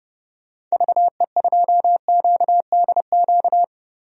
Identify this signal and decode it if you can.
Morse code, heard as a keyed tone, about 29 wpm, 700 Hz. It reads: VE2QDQ
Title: DXCC-100 Call Signs - 6 Characters - Repeated 2x - 30wpm